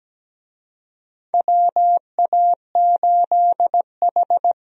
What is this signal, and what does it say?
Morse code, heard as a keyed tone, about 17 wpm, 700 Hz. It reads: WA8H
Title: USA Call Signs - 4 Characters - Repeated 3x - 17wpm